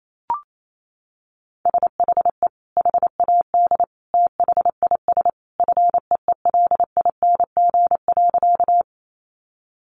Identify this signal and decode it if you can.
Morse code, heard as a keyed tone, about 28 wpm, 700 Hz. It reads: SHE HAD THIS FEELING.
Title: Sentences from Top 1200 Words 28wpm